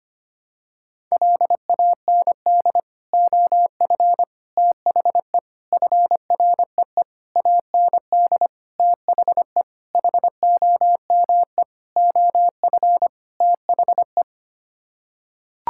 Morse code, heard as a keyed tone, about 25 wpm, 700 Hz. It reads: LAND OF THE FREE AND THE HOME OF THE
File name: Sets of 9 Words - 25wpm